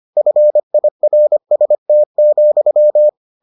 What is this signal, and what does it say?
Morse code, heard as a keyed tone, about 25 wpm, 600 Hz. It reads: FIRST,